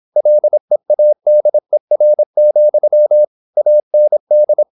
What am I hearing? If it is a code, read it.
Morse code, heard as a keyed tone, about 26 wpm, 600 Hz. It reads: LEADER, AND
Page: Never 26wpm | CW med Gnister